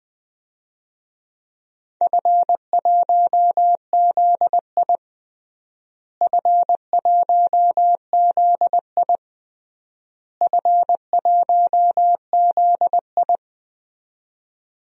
Morse code, heard as a keyed tone, about 20 wpm, 700 Hz. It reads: F1ZI F1ZI F1ZI
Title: DXCC-100 Call Signs - 4 Characters - Repeated 3x - 20wpm